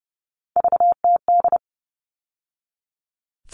Morse code, heard as a keyed tone, about 30 wpm, 700 Hz. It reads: VTD
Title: All Cut Numbers - 3-Digit 30wpm